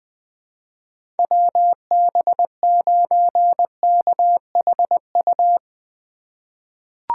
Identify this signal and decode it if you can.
Morse code, heard as a keyed tone, about 20 wpm, 700 Hz. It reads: WB9KHU
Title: USA Call Signs - 6 Characters - 2x Character Spacing - 20wpm